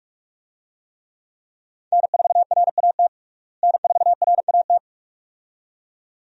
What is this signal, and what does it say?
Morse code, heard as a keyed tone, about 46 wpm, 700 Hz. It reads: N4RAT N4RAT
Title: USA Call Signs - 5 Characters - Repeated 2x - 45wpm